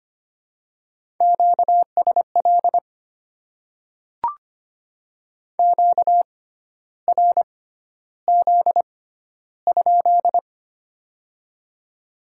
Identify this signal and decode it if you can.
Morse code, heard as a keyed tone, about 25 wpm, spaced as for 12 wpm, 700 Hz. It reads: QSL QRZ?
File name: Top 90 QSO Elements - 6x Character Spacing - 25wpm